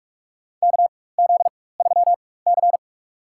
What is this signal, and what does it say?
Morse code, heard as a keyed tone, about 47 wpm, spaced as for 21 wpm, 690 Hz. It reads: KZ3C
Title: USA Call Signs - 4 Characters - 4x Character Spacing - 45wpm